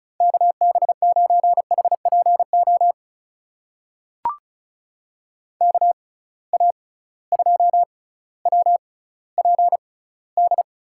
Morse code, heard as a keyed tone, about 36 wpm, spaced as for 18 wpm, 700 Hz. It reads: KB9HPO KA2WPD
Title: USA Call Signs - 6x Character Spacing - 35wpm